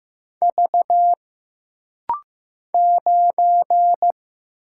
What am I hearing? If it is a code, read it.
Morse code, heard as a keyed tone, about 15 wpm, 700 Hz. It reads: V 9
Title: Single Letter-Number Alt - Rapid-Fire 15wpm